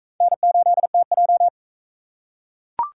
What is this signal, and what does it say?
Morse code, heard as a keyed tone, about 37 wpm, 700 Hz. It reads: N8TJ